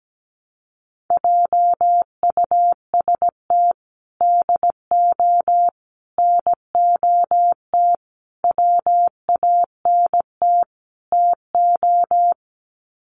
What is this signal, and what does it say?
Morse code, heard as a keyed tone, about 17 wpm, 700 Hz. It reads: JUST DO NOT WANT TO
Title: Sets of 5 Words Encore 17wpm